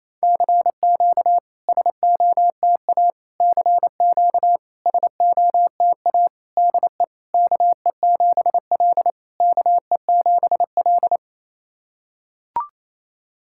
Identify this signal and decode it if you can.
Morse code, heard as a keyed tone, about 27 wpm, 690 Hz. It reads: CQ SOTA CQ SOTA DE KE7L KE7L
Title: SOTA QSO Practice 28wpm